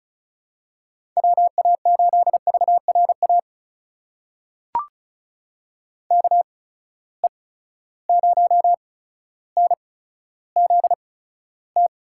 Morse code, heard as a keyed tone, about 36 wpm, spaced as for 15 wpm, 700 Hz. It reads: WA8VRA KE0NZT
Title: USA Call Signs - 6 Characters - 8x Character Spacing - 35wpm